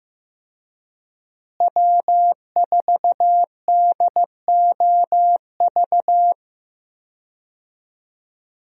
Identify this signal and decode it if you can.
Morse code, heard as a keyed tone, about 15 wpm, 700 Hz. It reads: W4DOV
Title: USA Call Signs - 5 Characters - Repeated 3x - 15wpm